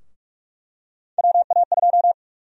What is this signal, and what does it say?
Morse code, heard as a keyed tone, about 46 wpm, 700 Hz. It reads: WA2
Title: USA Call Sign Prefixes - 45wpm